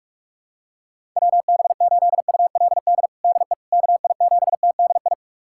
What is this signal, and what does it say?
Morse code, heard as a keyed tone, about 46 wpm, 700 Hz. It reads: WB8ULD DE KI7TDI
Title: USA Callsign DE Callsign - 6 Characters - 45wpm